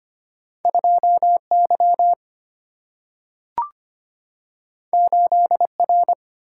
Morse code, heard as a keyed tone, about 25 wpm, 700 Hz. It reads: 2Y 8R
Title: Single-Letter Single-Number 25wpm